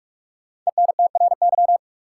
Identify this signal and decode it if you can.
Morse code, heard as a keyed tone, about 46 wpm, 700 Hz. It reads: ENTRY
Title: Sentences from Top 1500 Words - New Word Drill - 45wpm